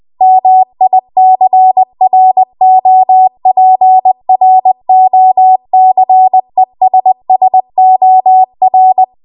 Morse code, heard as a keyed tone, about 20 wpm, 750 Hz. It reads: MICROPROCESSOR
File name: Archive / 20 WPM / 20WPM